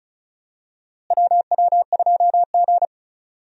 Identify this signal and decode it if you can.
Morse code, heard as a keyed tone, about 36 wpm, 700 Hz. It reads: WW2G
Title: USA Call Signs - 4 Characters - Repeated 2x - 35wpm